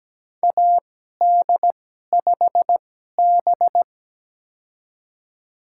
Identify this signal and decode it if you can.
Morse code, heard as a keyed tone, about 17 wpm, spaced as for 12 wpm, 700 Hz. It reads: AD5B
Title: USA Call Signs - 4 Characters - 2x Character Spacing - 17wpm